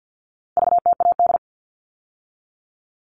Morse code, heard as a keyed tone, about 50 wpm, 700 Hz. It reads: VTAD
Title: All Cut Numbers - 4-Digit 50wpm